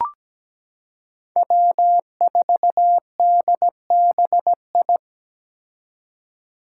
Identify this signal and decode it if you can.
Morse code, heard as a keyed tone, about 17 wpm, 700 Hz. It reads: W4DBI
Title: Five Challenging Letter-Number Combinations 17wpm